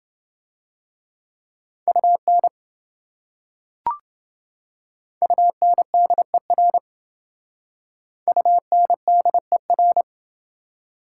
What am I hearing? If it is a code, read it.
Morse code, heard as a keyed tone, about 28 wpm, 700 Hz. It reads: UN UNDER UNDER